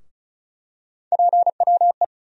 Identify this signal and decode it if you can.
Morse code, heard as a keyed tone, about 34 wpm, 700 Hz. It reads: PWE
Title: USA Call Sign Postfixes - 3 Letters 35wpm